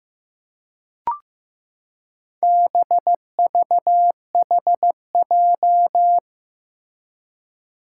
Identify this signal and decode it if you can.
Morse code, heard as a keyed tone, about 15 wpm, 700 Hz. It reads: BVHJ
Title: Four Challenging Letter-Number Combinations 15wpm